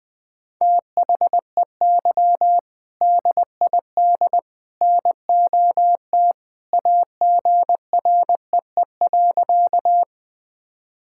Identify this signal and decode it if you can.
Morse code, heard as a keyed tone, about 21 wpm, 700 Hz. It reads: THEY DID NOT AGREE.
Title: Sentences with 3-Word Phrases - Spoken Beginning - 20wpm